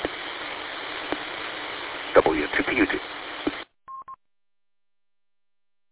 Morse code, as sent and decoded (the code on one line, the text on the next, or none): none